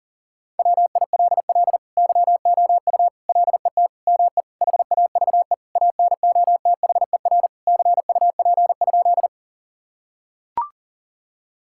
Morse code, heard as a keyed tone, about 40 wpm, 700 Hz. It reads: WILL YOU LET ME HAVE ANOTHER CUP?
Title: Sentences from Top 1600 Words 40wpm